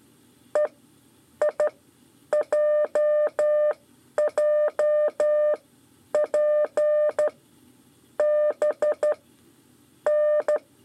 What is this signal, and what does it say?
Morse code, heard as a keyed tone, about 12 wpm, spaced as for 9 wpm, 600 Hz. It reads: EIJJP B N
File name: 5 LEZIONE RIPASSO 2